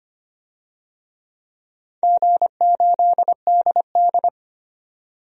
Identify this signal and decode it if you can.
Morse code, heard as a keyed tone, about 25 wpm, 700 Hz. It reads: G8DD
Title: DXCC-100 Call Signs - 4 Characters - 25wpm